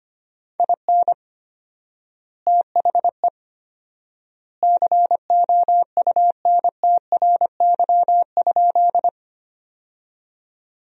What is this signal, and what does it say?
Morse code, heard as a keyed tone, about 25 wpm, 700 Hz. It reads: IN THE COUNTRY?
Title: 4x Word-Spacing - Sentences from Top 300 Words - 25wpm